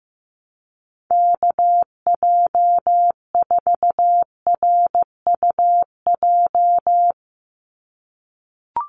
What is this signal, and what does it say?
Morse code, heard as a keyed tone, about 15 wpm, 700 Hz. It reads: KJ4RUJ
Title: USA Call Signs 15wpm